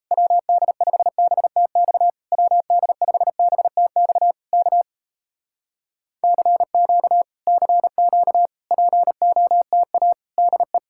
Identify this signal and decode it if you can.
Morse code, heard as a keyed tone, about 38 wpm, 700 Hz. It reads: WD5BTX WD5BTX K CQ CQ POTA DE